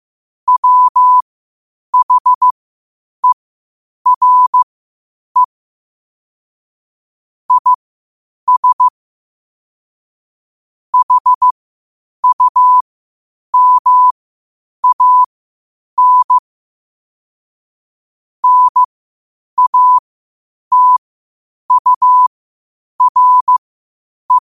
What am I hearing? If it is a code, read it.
Morse code, heard as a keyed tone, about 15 wpm, spaced as for 8 wpm, 1000 Hz. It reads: WHERE IS HUMAN NATURE
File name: qotdfiles / 8wpm